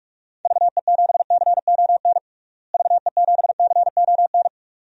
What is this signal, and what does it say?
Morse code, heard as a keyed tone, about 43 wpm, 700 Hz. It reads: VE7CON VE7CON
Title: DXCC-100 Call Signs - 6 Characters - Repeated 2x - 45wpm